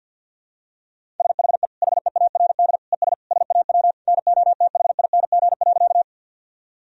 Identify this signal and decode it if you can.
Morse code, heard as a keyed tone, about 52 wpm, 700 Hz. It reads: SHE HEARD ES SAW NOTHING.
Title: Sentences from Top 500 Words and Abbreviations 50wpm